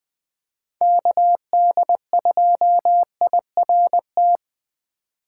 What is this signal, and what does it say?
Morse code, heard as a keyed tone, about 20 wpm, 700 Hz. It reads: KD2IRT
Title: USA Call Signs - 6 Characters - 4x Character Spacing - 20wpm